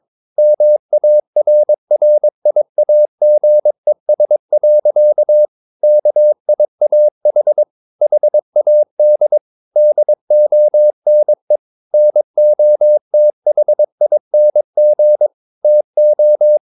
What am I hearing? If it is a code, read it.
Morse code, heard as a keyed tone, about 22 wpm, 600 Hz. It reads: MARRIAGES. KIAH HAD DONE NOTHING TO
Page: Romanen Never af Ken Follet på engelsk.